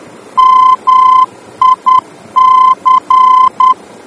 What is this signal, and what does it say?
Morse code, heard as a keyed tone, about 10 wpm, 1030 Hz. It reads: MIC
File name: germany / mx / MIC-D-VOR